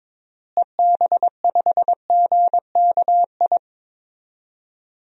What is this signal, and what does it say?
Morse code, heard as a keyed tone, about 22 wpm, 700 Hz. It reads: EB5GKI
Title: DXCC-100 Call Signs - 6 Characters - Repeated 2x - 22wpm